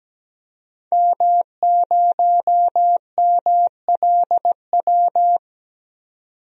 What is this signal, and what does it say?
Morse code, heard as a keyed tone, about 17 wpm, 700 Hz. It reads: M0MLW
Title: DXCC-100 Call Signs - 5 Characters - 17wpm